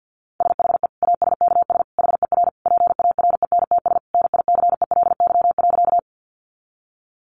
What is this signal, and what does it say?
Morse code, heard as a keyed tone, about 48 wpm, 700 Hz. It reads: SHE ASKS HER PARENTS NICELY.
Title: Easy Phrases 50wpm